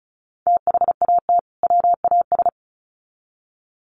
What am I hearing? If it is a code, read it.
Morse code, heard as a keyed tone, about 34 wpm, 700 Hz. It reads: THAT WAS
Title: Sets of 2 Words 35wpm